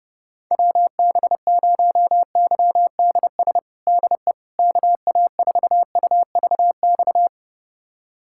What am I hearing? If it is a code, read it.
Morse code, heard as a keyed tone, about 30 wpm, 700 Hz. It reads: WB0YDS DE KA4UVX
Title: USA Callsign DE Callsign - 6 Characters - 30wpm